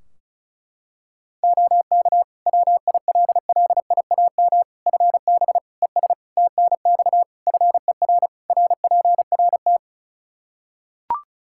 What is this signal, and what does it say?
Morse code, heard as a keyed tone, about 34 wpm, 700 Hz. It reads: OK WILLIAM FB ES TNX FER RPRT
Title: Practice QSO - Exchange 2 - Info - 35wpm